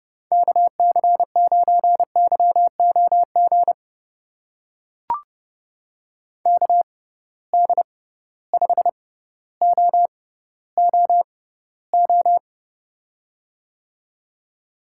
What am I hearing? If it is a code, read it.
Morse code, heard as a keyed tone, about 32 wpm, spaced as for 16 wpm, 700 Hz. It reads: KC9YOG KD5OOO
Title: USA Call Signs - 6 Characters - 6x Character Spacing - 30wpm